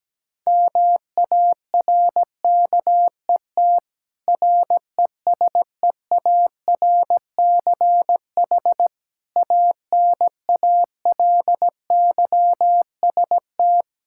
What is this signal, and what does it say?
Morse code, heard as a keyed tone, about 17 wpm, 700 Hz. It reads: MARKET RESEARCH ANALYST
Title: Practice QSO - Exchange 3 - More Info - 17wpm